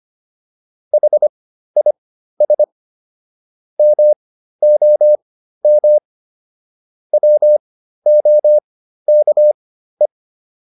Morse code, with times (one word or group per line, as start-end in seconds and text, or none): 0.9-2.6 HIS
3.8-6.0 MOM
7.1-10.1 WOKE